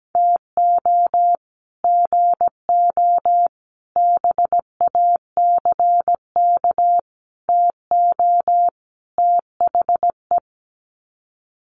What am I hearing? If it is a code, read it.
Morse code, heard as a keyed tone, about 17 wpm, 700 Hz. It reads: TO GO BACK TO THE
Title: Sets of 5 Words 17wpm